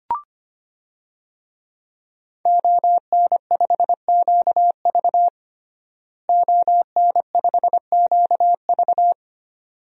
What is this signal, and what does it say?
Morse code, heard as a keyed tone, about 25 wpm, 700 Hz. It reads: ON5QV ON5QV
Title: DXCC-100 Call Signs - 5 Characters - Repeated 2x - 25wpm